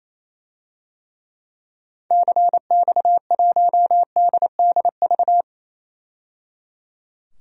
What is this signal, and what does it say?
Morse code, heard as a keyed tone, about 28 wpm, 700 Hz. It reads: CX1DDV
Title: DXCC-100 Call Signs - 6 Characters - 28wpm